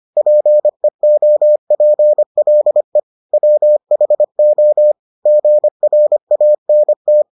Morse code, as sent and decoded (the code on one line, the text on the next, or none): .--. --- .--. .-.. / .-- .... --- / --. .-. .- -. -
PEOPLE WHO GRANT